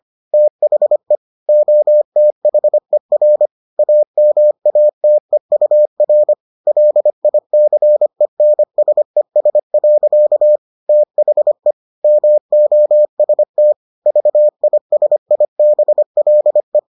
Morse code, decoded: THE OTHER AMATEUR LICENSES. THE MOST VISIBLE

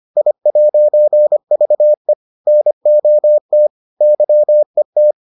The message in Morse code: I'VE NOT YET

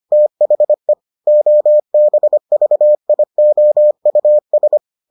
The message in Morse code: THE OBVIOUS

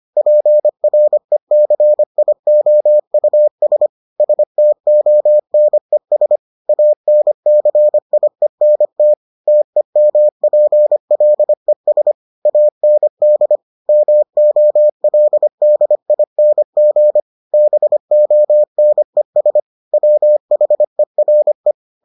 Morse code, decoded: PRECIOUS STONES ANCIENT TEMPLES AND MOLDING BONES WHERE

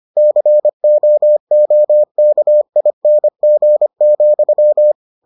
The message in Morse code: COOKING,

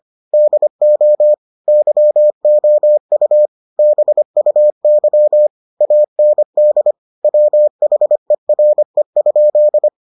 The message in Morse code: DO YOU BUY AND WHERE?

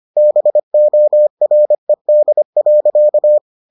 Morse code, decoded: BORED.

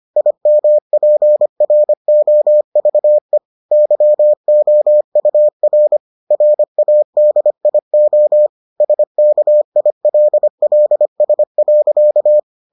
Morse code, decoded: IMPROVE YOUR RADIO SKILLS.

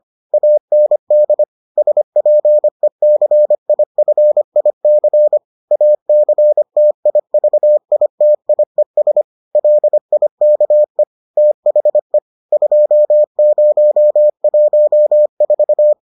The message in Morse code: AND SPECIFIC ACTIVITIES LIKE THE 2014

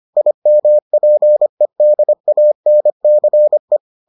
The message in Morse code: IMPEDANCE

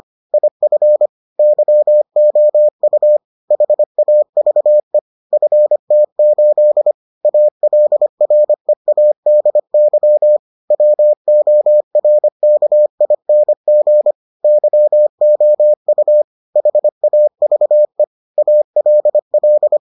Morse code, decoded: IF YOU HAVE FT8 ALREADY WORKING YOU HAVE ALL

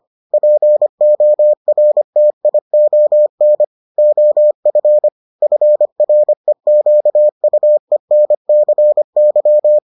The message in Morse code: PORTION OF FREQUENCY

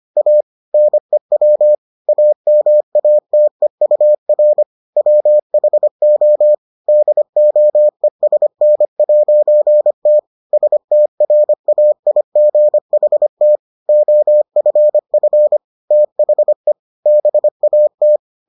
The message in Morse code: A NEW AMATEUR WHO DOESN'T STRAIGHT OFF THE BAT

